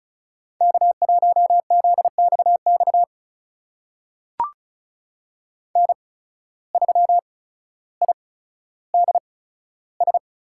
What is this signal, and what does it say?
Morse code, heard as a keyed tone, about 34 wpm, spaced as for 15 wpm, 700 Hz. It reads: K1ZXX N3IDS
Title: USA Call Signs - 5 Characters - 8x Character Spacing - 35wpm